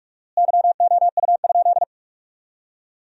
Morse code, heard as a keyed tone, about 42 wpm, 700 Hz. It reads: YOU?
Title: 4x Word-Spacing - Sentences from Top 100 Words - 45wpm